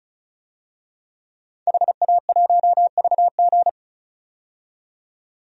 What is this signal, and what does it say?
Morse code, heard as a keyed tone, about 36 wpm, 700 Hz. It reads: HA1VG
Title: DXCC-100 Call Signs - 5 Characters - 35wpm